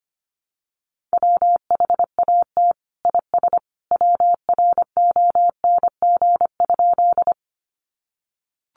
Morse code, heard as a keyed tone, about 25 wpm, 700 Hz. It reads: WHAT IS WRONG?
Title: Sentences from Top 900 Words 25wpm